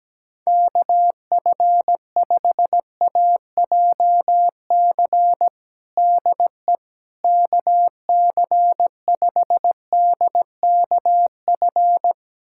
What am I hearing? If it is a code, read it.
Morse code, heard as a keyed tone, about 17 wpm, 700 Hz. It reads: KF5AJC DE KC5DKF
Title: USA Callsign DE Callsign - 6 Characters - 17wpm